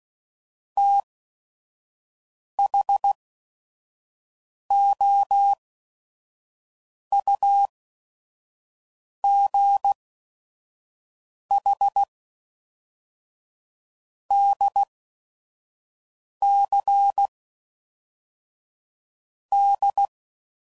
Morse code, heard as a keyed tone, about 16 wpm, spaced as for 5 wpm, 800 Hz. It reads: THOUGH DC D
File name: CW Code Groups @5WPM